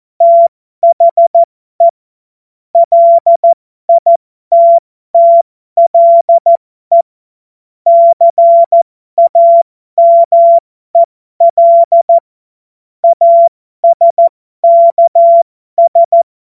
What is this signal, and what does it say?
Morse code, heard as a keyed tone, about 14 wpm, spaced as for 12 wpm, 680 Hz. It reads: THE LITTLE CAMEL ASKS
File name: Morsekurs / Witze